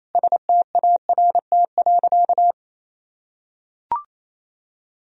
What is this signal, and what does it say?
Morse code, heard as a keyed tone, about 28 wpm, 700 Hz. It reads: START.
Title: Sentences from Top 1000 Words - Review Part 1 - 28wpm